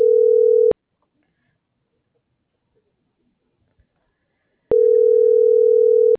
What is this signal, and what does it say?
Ringback tone (North America), 440 + 480 Hz.
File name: CENTRAL TERRESTRE / out karis - .77747